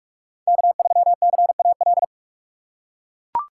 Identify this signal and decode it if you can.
Morse code, heard as a keyed tone, about 42 wpm, 700 Hz. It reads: K3CAL
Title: USA Call Signs - 5 Characters - 4x Character Spacing - 45wpm